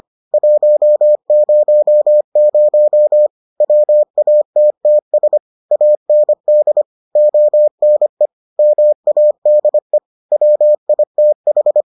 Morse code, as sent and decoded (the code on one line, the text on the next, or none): .---- ----- ----- / .-- .- - - ... / .- -. -.. / --- -. / -- .- -.. / .-- .. - ....
100 WATTS AND ONE MADE WITH